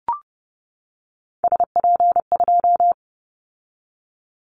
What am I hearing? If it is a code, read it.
Morse code, heard as a keyed tone, about 30 wpm, 700 Hz. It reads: SP2
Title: Callsign Prefixes for Top 25 DXCC - 30wpm